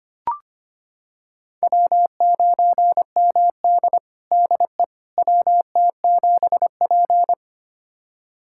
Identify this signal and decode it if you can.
Morse code, heard as a keyed tone, about 25 wpm, 700 Hz. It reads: W9MD DE WT7P
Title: USA Callsign DE Callsign - 4 Characters - 25wpm